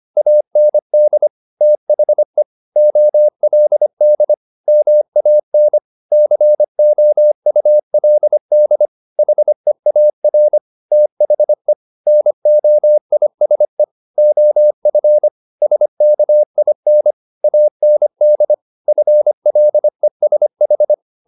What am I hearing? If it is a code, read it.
Morse code, heard as a keyed tone, about 25 wpm, 600 Hz. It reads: AND THE OLD MAN COULD HEAR THE NOISE OF SKIN AND FLESH